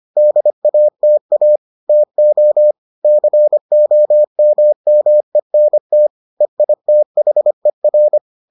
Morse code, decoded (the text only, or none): DATA TO COMMENT EITHER